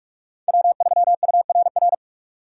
Morse code, heard as a keyed tone, about 43 wpm, 700 Hz. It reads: W3URR